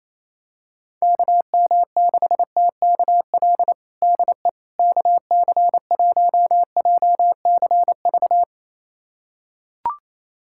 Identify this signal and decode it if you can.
Morse code, heard as a keyed tone, about 28 wpm, 700 Hz. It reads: KM6TKL DE KC1JCV
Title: USA Callsign DE Callsign 28wpm